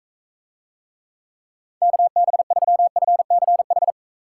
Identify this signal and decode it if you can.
Morse code, heard as a keyed tone, about 39 wpm, 700 Hz. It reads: KB3FCH